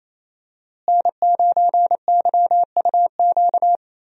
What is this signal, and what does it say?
Morse code, heard as a keyed tone, about 28 wpm, 700 Hz. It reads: N9YUQ
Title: USA Call Signs - 5 Characters - Repeated 2x - 28wpm